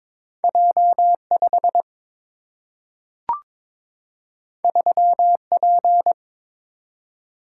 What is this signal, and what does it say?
Morse code, heard as a keyed tone, about 22 wpm, 700 Hz. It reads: J 5 3 P
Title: Single-Letter Single-Number 22wpm